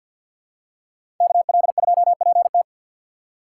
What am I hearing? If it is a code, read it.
Morse code, heard as a keyed tone, about 47 wpm, 700 Hz. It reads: KL2PT
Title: USA Call Signs - 4x Character Spacing - 50wpm